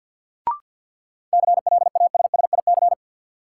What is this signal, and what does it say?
Morse code, heard as a keyed tone, about 50 wpm, 700 Hz. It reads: CLASSIC
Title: Sentences from Top 1000 Words - New Word Drill - 50wpm